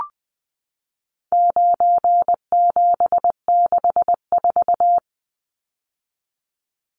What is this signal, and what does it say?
Morse code, heard as a keyed tone, about 20 wpm, 700 Hz. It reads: 9764
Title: Numbers - 4-Digit 20wpm